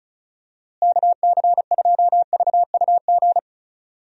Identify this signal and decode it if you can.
Morse code, heard as a keyed tone, about 34 wpm, 700 Hz. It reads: KC2VUG
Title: USA Call Signs - 6 Characters - Repeated 3x - 35wpm